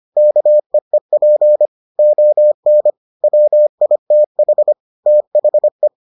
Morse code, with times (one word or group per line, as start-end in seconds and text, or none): 0.2-1.7 KEEP
2.0-2.9 ON
3.2-4.7 WITH
5.1-5.9 THE